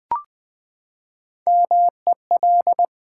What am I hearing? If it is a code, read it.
Morse code, heard as a keyed tone, about 20 wpm, 700 Hz. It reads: MEL
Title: Build A Word - 3 Letters At A Time - 6-Letter Words - 20wpm